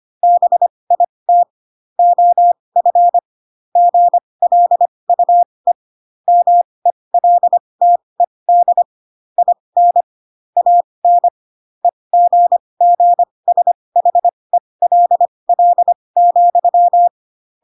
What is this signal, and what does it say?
Morse code, heard as a keyed tone, about 25 wpm, spaced as for 20 wpm, 700 Hz. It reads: BIT OF GLUE MELTED IN AN EGGSHELL,
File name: Pinocchio / Pinocchio ch8 15 25 ik1hge